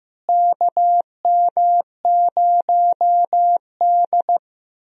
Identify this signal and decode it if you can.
Morse code, heard as a keyed tone, about 15 wpm, 700 Hz. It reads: KM0D
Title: USA Call Signs - 4 Characters - Repeated 3x - 15wpm